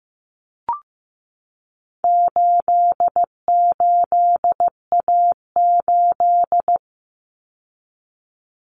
Morse code, heard as a keyed tone, about 15 wpm, 700 Hz. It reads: 88A8
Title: Common Cut Numbers - 4-Digit 15wpm